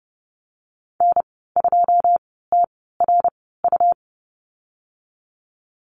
Morse code, heard as a keyed tone, about 30 wpm, spaced as for 17 wpm, 700 Hz. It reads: N2TRU
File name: Lesson 19g - Letter w - Call Signs 30wpm